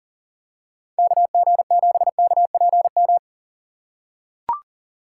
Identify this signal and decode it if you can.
Morse code, heard as a keyed tone, about 37 wpm, 700 Hz. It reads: KG7KPM
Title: USA Call Signs - 6 Characters - 3x Character Spacing - 40wpm